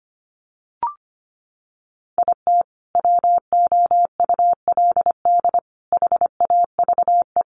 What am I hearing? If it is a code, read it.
Morse code, heard as a keyed tone, about 25 wpm, 700 Hz. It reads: IT WOULD HAVE
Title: Sets of 4 Words 25wpm